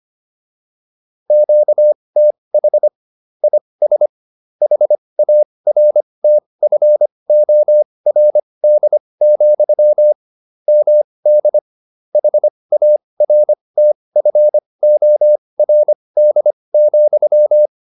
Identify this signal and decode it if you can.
Morse code, heard as a keyed tone, about 25 wpm, spaced as for 20 wpm, 600 Hz. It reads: QTH IS HARTFORD, MD HARTFORD,
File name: QSO401a 20